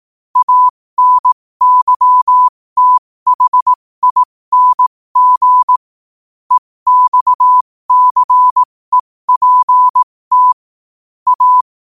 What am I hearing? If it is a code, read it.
Morse code, heard as a keyed tone, about 18 wpm, spaced as for 15 wpm, 1000 Hz. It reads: ANYTHING EXCEPT A
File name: qotdfiles / 15wpm